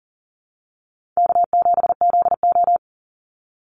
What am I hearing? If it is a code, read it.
Morse code, heard as a keyed tone, about 40 wpm, 700 Hz. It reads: K7ZO
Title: NA SOTA Chaser Callsigns 40wpm